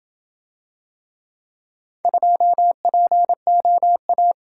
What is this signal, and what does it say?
Morse code, heard as a keyed tone, about 26 wpm, 700 Hz. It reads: 2POA